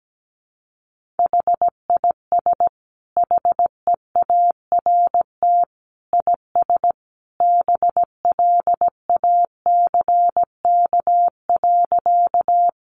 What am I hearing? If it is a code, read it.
Morse code, heard as a keyed tone, about 17 wpm, 700 Hz. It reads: HIS HEART IS BLACK.
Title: Sentences from Top 1200 Words 17wpm